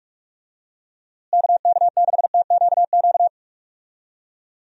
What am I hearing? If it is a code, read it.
Morse code, heard as a keyed tone, about 43 wpm, 700 Hz. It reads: KK6TQQ
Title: USA Call Signs - 6 Characters - Repeated 3x - 45wpm